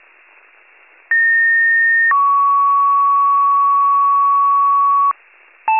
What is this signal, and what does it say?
Two-tone page, heard as a one-second tone, then a three-second tone, about 1850 then 1120 Hz, with noise. It is heard from 1.1 to 5.1 s.